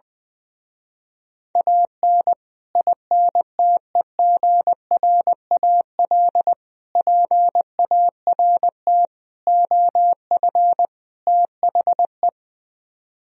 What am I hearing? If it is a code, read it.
Morse code, heard as a keyed tone, about 20 wpm, 700 Hz. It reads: AN INTEGRAL PART OF THE
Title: Speed-Racing - Sets of 5 Words 20wpm